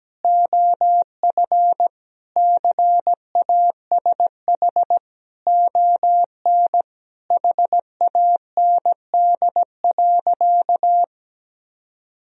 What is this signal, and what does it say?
Morse code, heard as a keyed tone, about 17 wpm, 700 Hz. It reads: OF CASH ON HAND.
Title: Sentences from Top 1000 Words - Review Part 4 - 17wpm